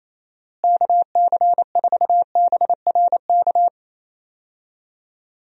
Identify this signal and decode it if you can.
Morse code, heard as a keyed tone, about 27 wpm, 700 Hz. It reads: KC4BRK
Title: USA Call Signs - 6 Characters - Repeated 2x - 28wpm